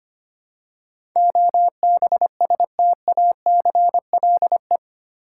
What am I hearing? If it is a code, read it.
Morse code, heard as a keyed tone, about 25 wpm, 700 Hz. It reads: OBSTACLE
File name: 8 Letter Words - First 3 Letters Given - 25wpm